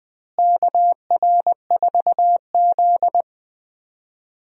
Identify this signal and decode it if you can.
Morse code, heard as a keyed tone, about 20 wpm, 700 Hz. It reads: KR4Z
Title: USA Call Signs - 4 Characters - Repeated 3x - 20wpm